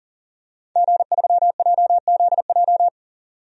Morse code, heard as a keyed tone, about 40 wpm, 700 Hz. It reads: G3JZJ
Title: DXCC-100 Call Signs - 5 Characters - 40wpm